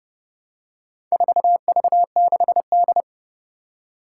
Morse code, heard as a keyed tone, about 30 wpm, 700 Hz. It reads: 4V6D
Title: Four Challenging Letter-Number Combinations 30wpm